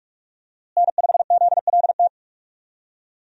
Morse code, heard as a keyed tone, about 44 wpm, 700 Hz. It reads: N5ZLT
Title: USA Call Signs - 5 Characters - Repeated 2x - 45wpm